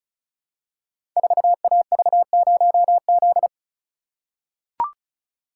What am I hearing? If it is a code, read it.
Morse code, heard as a keyed tone, about 36 wpm, 690 Hz. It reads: 4AV0Z
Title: Code Groups - 5 Letter:Numbers - 35wpm